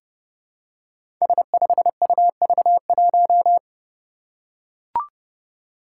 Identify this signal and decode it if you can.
Morse code, heard as a keyed tone, about 30 wpm, 700 Hz. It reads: S5UV1
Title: Five Challenging Letter-Number Combinations 30wpm